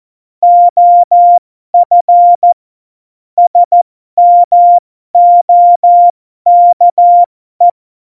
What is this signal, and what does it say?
Morse code, heard as a keyed tone, about 14 wpm, spaced as for 12 wpm, 700 Hz. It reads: OF SMOKE
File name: Morsekurs / Witze